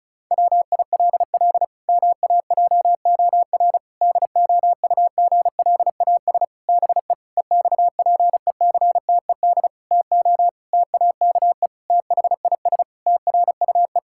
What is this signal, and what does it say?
Morse code, heard as a keyed tone, about 36 wpm, 700 Hz. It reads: WILL MAJOR DOUGLAS BE EXPECTED TO TAKE THIS TRUE
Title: Pangram Challenge v2 - 35wpm